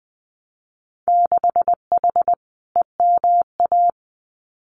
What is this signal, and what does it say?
Morse code, heard as a keyed tone, about 20 wpm, 700 Hz. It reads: 6H EMA
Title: Winter Field Day Exchanges Encore 20wpm